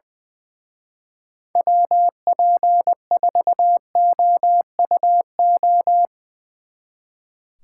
Morse code, heard as a keyed tone, about 20 wpm, 700 Hz. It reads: WP4OUO
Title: Speed Racing - USA 2x3 Call Signs 20wpm